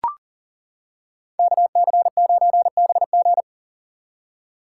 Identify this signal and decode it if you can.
Morse code, heard as a keyed tone, about 40 wpm, 700 Hz. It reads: KC9BG
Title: K1USN SST QSO Practice - 40wpm